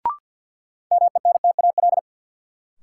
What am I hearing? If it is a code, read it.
Morse code, heard as a keyed tone, about 52 wpm, 700 Hz. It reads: MENTAL
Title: Sentences from Top 1000 Words - Review Part 4 - New Word Drill - 50wpm